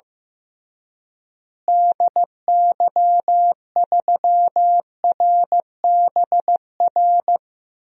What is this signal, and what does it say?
Morse code, heard as a keyed tone, about 15 wpm, 700 Hz. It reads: DY3RBR
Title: DXCC-100 Call Signs - 6 Characters - Repeated 2x - 15wpm